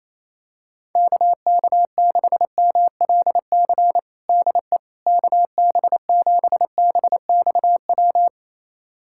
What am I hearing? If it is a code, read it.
Morse code, heard as a keyed tone, about 27 wpm, 700 Hz. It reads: KK6MLC DE KB7BXW
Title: USA Callsign DE Callsign 28wpm